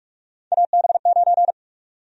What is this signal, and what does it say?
Morse code, heard as a keyed tone, about 44 wpm, 700 Hz. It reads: AB9
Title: USA Call Sign Prefixes - Repeated 2x - 45wpm